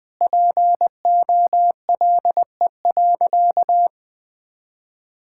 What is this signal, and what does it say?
Morse code, heard as a keyed tone, about 20 wpm, 700 Hz. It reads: POLE.
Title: Sentences with 5-Word Phrases - 20wpm